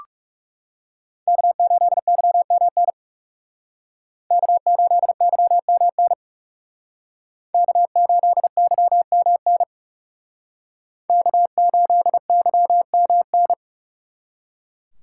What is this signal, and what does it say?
Morse code, heard as a keyed tone, about 38 wpm, 700 Hz. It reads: K8YMN K8YMN K8YMN K8YMN